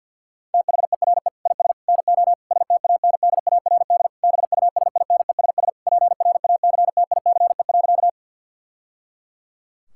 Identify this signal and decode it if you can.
Morse code, heard as a keyed tone, about 50 wpm, 700 Hz. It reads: THERE IS NO STANDARD BUSINESS PRACTICE.
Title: Sentences from Top 1000 Words - Review Part 6 - 50wpm